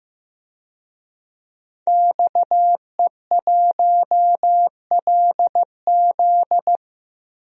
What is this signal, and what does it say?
Morse code, heard as a keyed tone, about 15 wpm, 700 Hz. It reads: XE1LZ
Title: DXCC-100 Call Signs - 5 Characters - Repeated 2x - 15wpm